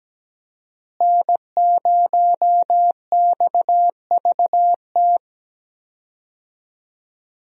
Morse code, heard as a keyed tone, about 17 wpm, 700 Hz. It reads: N0XVT
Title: USA Call Signs - 5 Characters - Repeated 2x - 17wpm